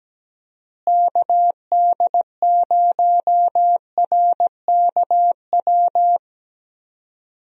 Morse code, heard as a keyed tone, about 17 wpm, 700 Hz. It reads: KD0RKW
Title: USA Call Signs - 6 Characters - Repeated 3x - 17wpm